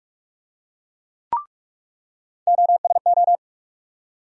Morse code, heard as a keyed tone, about 47 wpm, 700 Hz. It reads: OSO
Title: USA Call Sign Postfixes - 3 Letters 45wpm